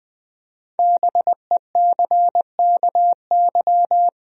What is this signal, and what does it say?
Morse code, heard as a keyed tone, about 20 wpm, 700 Hz. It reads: BECKY
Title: USA Common Names - Max 5 Letters - 20wpm